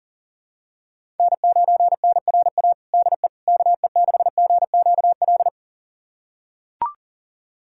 Morse code, heard as a keyed tone, about 37 wpm, 700 Hz. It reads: N9NRA DE KE6GQL
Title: USA Callsign DE Callsign 40wpm